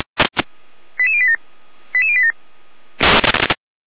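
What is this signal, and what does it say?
Five-tone selective call (ZVEI), heard as quick runs of five tones, tones of about 80 ms. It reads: 88087 88087
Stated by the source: ZVEI five-tone selective call received by SDR.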